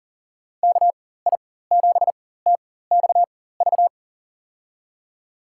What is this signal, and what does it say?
Morse code, heard as a keyed tone, about 40 wpm, spaced as for 19 wpm, 700 Hz. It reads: KI7TXV
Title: USA Call Signs - 6 Characters - 4x Character Spacing - 40wpm